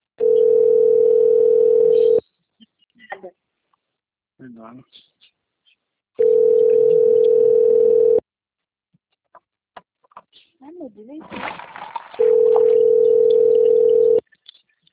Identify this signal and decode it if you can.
Ringback tone (North America), 440 + 480 Hz, with speech.
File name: rg-1600-unknown- - - .24392